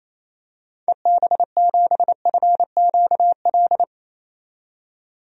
Morse code, heard as a keyed tone, about 28 wpm, 700 Hz. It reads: EB7FQL
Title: DXCC-100 Call Signs - 6 Characters - 28wpm